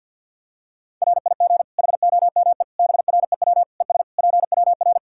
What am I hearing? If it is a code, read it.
Morse code, heard as a keyed tone, about 52 wpm, 700 Hz. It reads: RIG HOME BREW ES PWR
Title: Practice QSO - Exchange 2 - Info - 50wpm